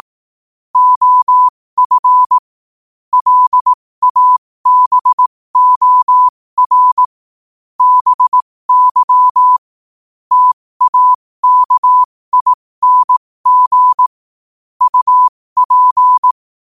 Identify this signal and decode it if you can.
Morse code, heard as a keyed tone, about 18 wpm, spaced as for 15 wpm, 1000 Hz. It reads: OF LABOR BY TAKING UP
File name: qotdfiles / 15wpm